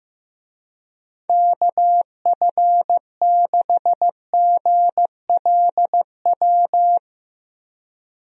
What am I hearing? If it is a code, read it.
Morse code, heard as a keyed tone, about 15 wpm, 700 Hz. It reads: KF6GLW
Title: USA Call Signs - 6 Characters - 15wpm